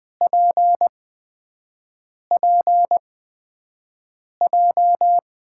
Morse code, heard as a keyed tone, about 20 wpm, spaced as for 5 wpm, 700 Hz. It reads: PPJ
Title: USA Call Signs - 5 Characters - 8x Character Spacing - 20wpm